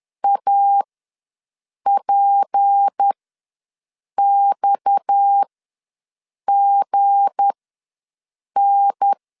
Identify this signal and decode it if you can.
Morse code, heard as a keyed tone, about 10 wpm, spaced as for 6 wpm, 800 Hz. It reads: APXGN
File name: LEZIONI / 7 GF9Z / 7 G ws 10wpm 3843